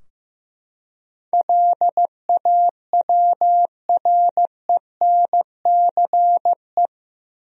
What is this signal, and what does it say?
Morse code, heard as a keyed tone, about 15 wpm, 700 Hz. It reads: LAWRENCE
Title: USA Common Names - 15wpm